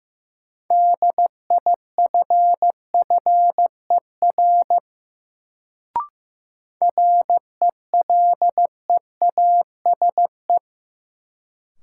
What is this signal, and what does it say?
Morse code, heard as a keyed tone, about 15 wpm, 700 Hz. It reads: DIFFER RELEASE
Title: Sentences from Top 1100 Words - New Word Drill Alt - 15wpm